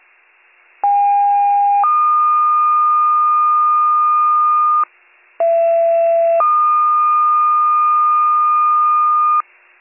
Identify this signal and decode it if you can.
Two-tone page, heard as a one-second tone, then a three-second tone, about 810 then 1220 Hz, about 670 then 1220 Hz. It is heard from 0.8 to 9.4 s.